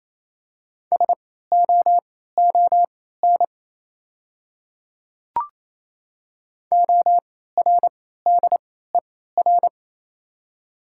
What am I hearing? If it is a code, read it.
Morse code, heard as a keyed tone, about 28 wpm, spaced as for 16 wpm, 700 Hz. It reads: SOON ORDER
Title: 3x Character Spacing - Top 500 Words - 28wpm